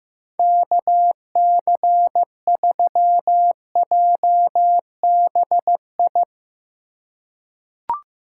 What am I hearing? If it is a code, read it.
Morse code, heard as a keyed tone, about 16 wpm, 700 Hz. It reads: KC3JBI
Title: USA Call Signs - 6 Characters - 3x Character Spacing - 15wpm